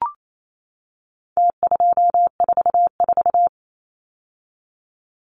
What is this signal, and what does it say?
Morse code, heard as a keyed tone, about 28 wpm, 700 Hz. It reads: T244
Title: Numbers - Contest Style - 4-Digit 28wpm